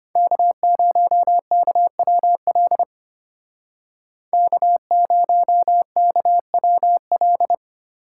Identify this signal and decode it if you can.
Morse code, heard as a keyed tone, about 32 wpm, 700 Hz. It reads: K0KWL K0KWL